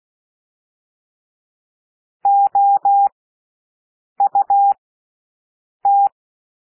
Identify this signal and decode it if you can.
Morse code, heard as a keyed tone, about 16 wpm, spaced as for 6 wpm, 800 Hz. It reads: OUT
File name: cwmp3 / W1 SOUND WORDS 16 wpm